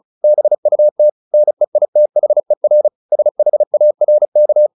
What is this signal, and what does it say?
Morse code, heard as a keyed tone, about 34 wpm, 600 Hz. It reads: BUT NEITHER SHARK
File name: the old man and the sea 2 35wpm